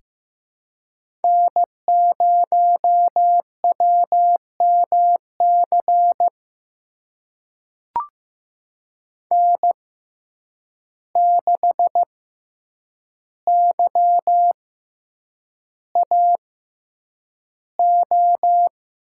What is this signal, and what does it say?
Morse code, heard as a keyed tone, about 15 wpm, spaced as for 8 wpm, 700 Hz. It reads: N0WMC N6YAO
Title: USA Call Signs - 5 Characters - 6x Character Spacing - 15wpm